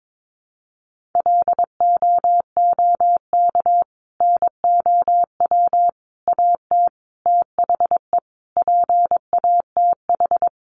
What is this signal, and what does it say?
Morse code, heard as a keyed tone, about 22 wpm, 700 Hz. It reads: LOOK NOW AT THE PATH
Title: Sentences from Top 1400 Words 22wpm